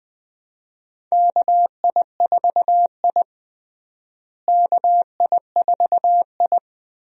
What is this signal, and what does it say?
Morse code, heard as a keyed tone, about 20 wpm, 700 Hz. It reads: KI4I KI4I
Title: USA Call Signs - 4 Characters - Repeated 2x - 20wpm